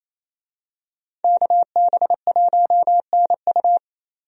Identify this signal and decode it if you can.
Morse code, heard as a keyed tone, about 28 wpm, 700 Hz. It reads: KB1NU
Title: USA Call Signs - 5 Characters - 28wpm